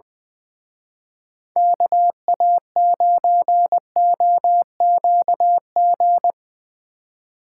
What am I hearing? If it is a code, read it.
Morse code, heard as a keyed tone, about 23 wpm, 700 Hz. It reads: KA9OQG
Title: Speed Racing - USA 2x3 Call Signs 20wpm